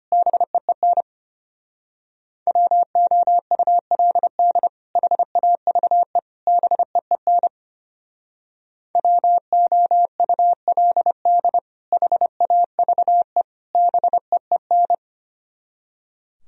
Morse code, heard as a keyed tone, about 30 wpm, 700 Hz. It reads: BEEN WOULD HAVE BEEN WOULD HAVE BEEN
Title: Speed-Racing - Sets of 3 Words 25wpm